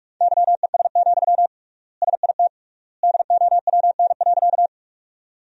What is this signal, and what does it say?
Morse code, heard as a keyed tone, about 46 wpm, 700 Hz. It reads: YES, SIT DOWN.
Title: 3x Word-Spacing - Sentences from Top 500 Words - 45wpm